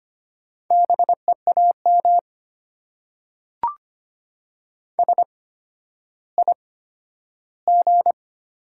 Morse code, heard as a keyed tone, about 25 wpm, spaced as for 12 wpm, 690 Hz. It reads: BEAM SIG
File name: Top 90 QSO Elements - 8x Character Spacing - 25wpm